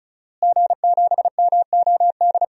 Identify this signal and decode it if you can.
Morse code, heard as a keyed tone, about 34 wpm, 700 Hz. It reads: G7MOD
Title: DXCC-100 Call Signs - 5 Characters - Repeated 3x - 35wpm